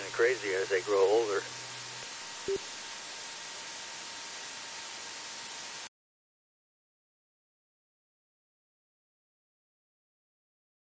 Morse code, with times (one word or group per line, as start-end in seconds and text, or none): none